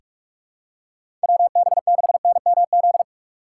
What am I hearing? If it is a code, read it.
Morse code, heard as a keyed tone, about 41 wpm, 700 Hz. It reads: WB6NMZ
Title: USA Call Signs - 6 Characters - 4x Character Spacing - 45wpm